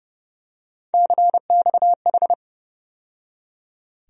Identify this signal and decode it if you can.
Morse code, heard as a keyed tone, about 30 wpm, 700 Hz. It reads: CXH
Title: USA Call Sign Postfixes - 3 Letters 30wpm